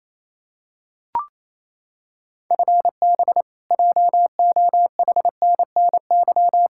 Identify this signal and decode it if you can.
Morse code, heard as a keyed tone, about 28 wpm, 700 Hz. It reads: FB JOHNNY
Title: Practice QSO - Exchange 3 - More Info - 28wpm